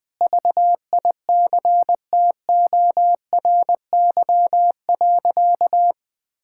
Morse code, heard as a keyed tone, about 21 wpm, 700 Hz. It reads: VICTORY.
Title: Sentences with 9-Word Phrases - Spoken Beginning - 20wpm